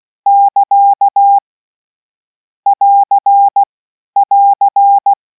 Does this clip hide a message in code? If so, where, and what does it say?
Yes: Morse code from 0.3 to 5.1 s, heard as a keyed tone, about 16 wpm, 800 Hz. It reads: <KA> <AR> <AR>